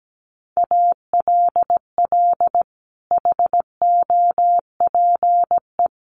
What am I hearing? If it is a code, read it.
Morse code, heard as a keyed tone, about 17 wpm, 700 Hz. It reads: ALL HOPE
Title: Sentences from Top 1400 Words 17wpm